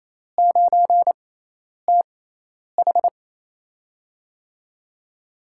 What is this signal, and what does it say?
Morse code, heard as a keyed tone, about 28 wpm, spaced as for 10 wpm, 700 Hz. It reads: 9TH
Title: USA Call Signs - 6x Character Spacing - 28wpm